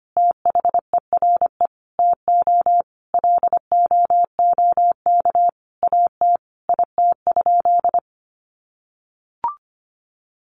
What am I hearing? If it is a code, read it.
Morse code, heard as a keyed tone, about 25 wpm, 700 Hz. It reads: THERE TO LOOK AT IT?
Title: Sentences from Top 100 Words 25wpm